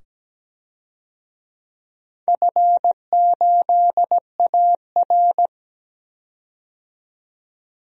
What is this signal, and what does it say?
Morse code, heard as a keyed tone, about 17 wpm, 700 Hz. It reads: F8AR
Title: DXCC-100 Call Signs - 4 Characters - 17wpm